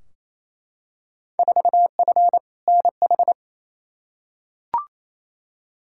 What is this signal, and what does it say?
Morse code, heard as a keyed tone, about 28 wpm, 700 Hz. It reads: 4F NH
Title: Field Day Exchanges Encore - 28wpm